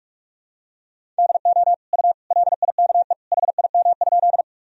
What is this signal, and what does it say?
Morse code, heard as a keyed tone, about 46 wpm, 700 Hz. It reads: DO U LIKE HIM?
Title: Sentences from Top 100 Words and Abbreviations 45wpm